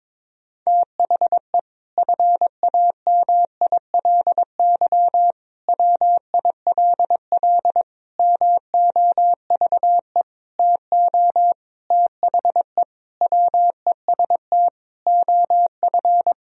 Morse code, heard as a keyed tone, about 22 wpm, 700 Hz. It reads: THE FAMILY WILL MOVE TO THE WEST OF
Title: Sentences from Top 1000 Words - Review Part 2 - 22wpm